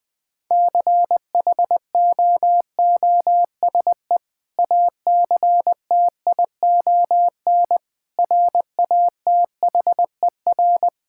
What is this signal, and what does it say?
Morse code, heard as a keyed tone, about 20 wpm, 700 Hz. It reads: CHOOSE ACTION RATHER
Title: Sentences with 2-Word Phrases - Spoken Beginning - 20wpm